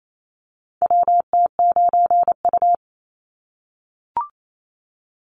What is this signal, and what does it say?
Morse code, heard as a keyed tone, about 28 wpm, 700 Hz. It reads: WT9U
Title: Call Signs - 28wpm